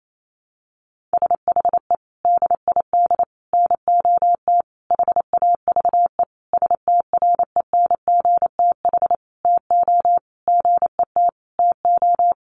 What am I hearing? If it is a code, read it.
Morse code, heard as a keyed tone, about 28 wpm, 700 Hz. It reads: SHE DID NOT HAVE STRENGTH TO GET TO
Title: Sentences from Top 1300 Words 28wpm